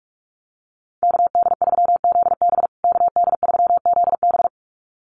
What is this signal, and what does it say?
Morse code, heard as a keyed tone, about 44 wpm, 700 Hz. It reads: KD3ZB KD3ZB
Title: USA Call Signs Repeated 2x 45wpm